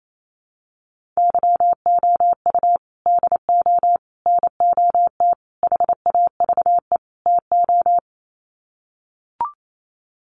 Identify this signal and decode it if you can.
Morse code, heard as a keyed tone, about 28 wpm, 700 Hz. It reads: YOU DO NOT HAVE TO
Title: Sets of 5 Words 28wpm